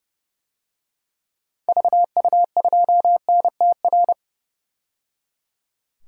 Morse code, heard as a keyed tone, about 30 wpm, 700 Hz. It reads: VU2NTR
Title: DXCC-100 Call Signs - 6 Characters - 30wpm